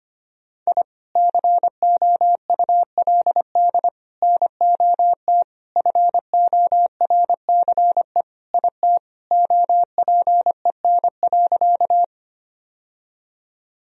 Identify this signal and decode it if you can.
Morse code, heard as a keyed tone, about 25 wpm, 700 Hz. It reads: I COULD NOT FORCE IT OPEN.
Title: Sentences from Top 1000 Words - Review Part 3 - 25wpm